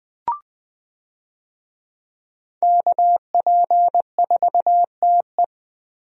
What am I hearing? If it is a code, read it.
Morse code, heard as a keyed tone, about 20 wpm, 700 Hz. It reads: KP4TE
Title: DXCC-100 Call Signs - 5 Characters - Repeated 2x - 20wpm